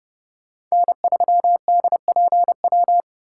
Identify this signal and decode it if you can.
Morse code, heard as a keyed tone, about 35 wpm, 700 Hz. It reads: N3DPW
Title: Speed Racing - USA 1x3 Call Signs 30wpm